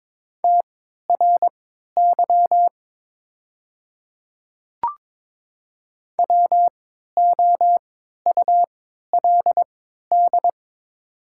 Morse code, heard as a keyed tone, about 22 wpm, spaced as for 13 wpm, 700 Hz. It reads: TRY WOULD
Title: 3x Character Spacing - Top 200 Words - 22wpm